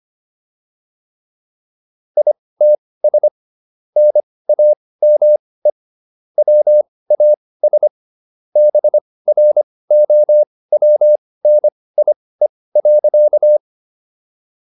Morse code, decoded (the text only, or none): ITS NAME WAS BROWNIE.